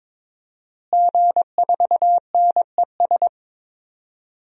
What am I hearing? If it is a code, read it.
Morse code, heard as a keyed tone, about 22 wpm, 700 Hz. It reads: G4NES
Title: DXCC-100 Call Signs - 5 Characters - Repeated 2x - 22wpm